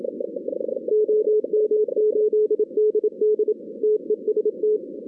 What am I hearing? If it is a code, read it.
Morse code, heard as a keyed tone, about 27 wpm, 430 Hz. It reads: OM8DD TEST
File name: 2016 arrl dx cw / 2016-ARRL-DX-CW-K5TR- - 0-41